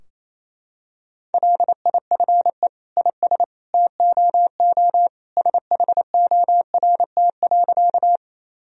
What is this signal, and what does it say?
Morse code, heard as a keyed tone, about 28 wpm, 700 Hz. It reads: LIFE IS TOO SHORT.
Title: Sentences from Top 1000 Words - Review Part 2 - 28wpm